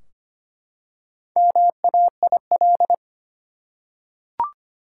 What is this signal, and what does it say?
Morse code, heard as a keyed tone, about 25 wpm, 700 Hz. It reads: MAIL
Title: Four Letter Words - 25wpm